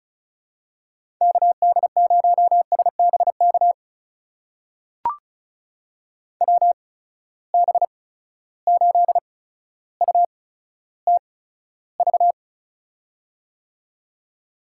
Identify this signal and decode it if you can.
Morse code, heard as a keyed tone, about 36 wpm, spaced as for 15 wpm, 700 Hz. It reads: KD0SBK WB8UTV
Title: USA Call Signs - 6 Characters - 8x Character Spacing - 35wpm